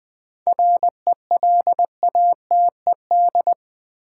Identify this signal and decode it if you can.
Morse code, heard as a keyed tone, about 20 wpm, 700 Hz. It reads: RELATED
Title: Sentences from Top 1000 Words - Review Part 5 - New Word Drill - 20wpm